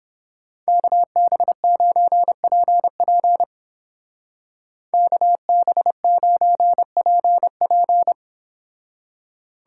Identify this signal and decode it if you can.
Morse code, heard as a keyed tone, about 31 wpm, 700 Hz. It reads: KB9PP KB9PP